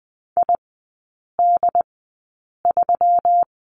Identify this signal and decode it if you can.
Morse code, heard as a keyed tone, about 20 wpm, spaced as for 8 wpm, 700 Hz. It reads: ID3
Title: 121 Lesson 20e - Number 3 - Three Characters 20wpm